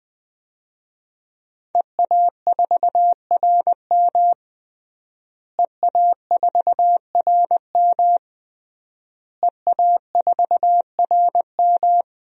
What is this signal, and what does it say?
Morse code, heard as a keyed tone, about 20 wpm, 700 Hz. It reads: EA4RM EA4RM EA4RM
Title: DXCC-100 Call Signs - 5 Characters - Repeated 3x - 20wpm